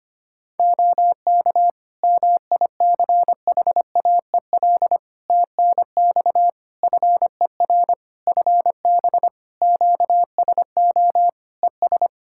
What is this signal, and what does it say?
Morse code, heard as a keyed tone, about 25 wpm, 700 Hz. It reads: OK MICHAEL TNX FER FB QSO ES
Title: K1USN SST QSO Practice 25wpm